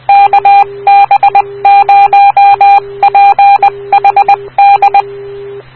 Morse code, heard as a keyed tone, about 20 wpm, 800 Hz. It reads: KB0PHD
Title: Rocky Mountain Radio League Repeaters